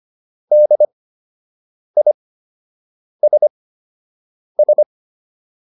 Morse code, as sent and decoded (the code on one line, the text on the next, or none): -.. .. ... ...
DISS